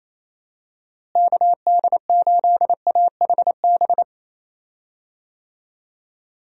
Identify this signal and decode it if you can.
Morse code, heard as a keyed tone, about 28 wpm, 700 Hz. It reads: KD8AHB
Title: USA Call Signs - 6 Characters - Repeated 2x - 28wpm